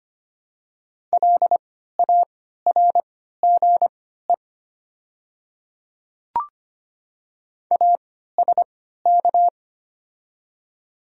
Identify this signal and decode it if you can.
Morse code, heard as a keyed tone, about 25 wpm, spaced as for 14 wpm, 690 Hz. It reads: LARGE ASK
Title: 3x Character Spacing - Top 200 Words - 25wpm